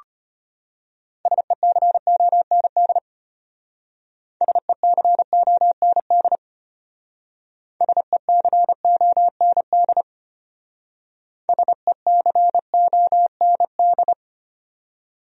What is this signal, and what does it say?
Morse code, heard as a keyed tone, about 32 wpm, 700 Hz. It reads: SECOND SECOND SECOND SECOND